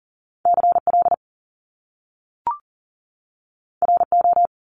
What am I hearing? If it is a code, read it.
Morse code, heard as a keyed tone, about 38 wpm, 700 Hz. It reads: CL RO
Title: Top Two Letter Combinations in Words 40wpm